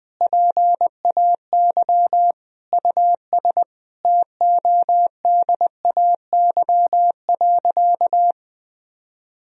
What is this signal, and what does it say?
Morse code, heard as a keyed tone, about 20 wpm, 700 Hz. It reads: PAY US TODAY.
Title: Sentences from Top 1000 Words - Review Part 4 - 20wpm